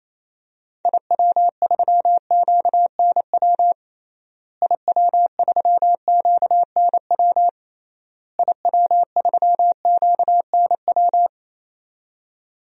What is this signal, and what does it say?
Morse code, heard as a keyed tone, about 28 wpm, 700 Hz. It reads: IW3QNW IW3QNW IW3QNW
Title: DXCC-100 Call Signs - 6 Characters - Repeated 3x - 28wpm